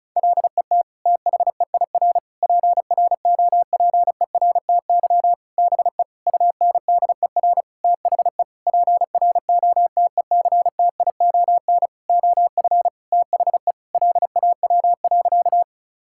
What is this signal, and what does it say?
Morse code, heard as a keyed tone, about 34 wpm, 700 Hz. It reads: LET THEIR PROPERTY BE UNDER THE PROTECTION OF THE LAW.
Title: Sentences from Top 1600 Words 35wpm